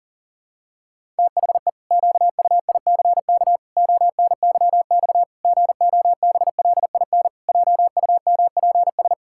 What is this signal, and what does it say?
Morse code, heard as a keyed tone, about 40 wpm, 700 Hz. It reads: THE QUICK ONYX GOBLIN JUMPS
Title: Pangram Challenge v2 - 40wpm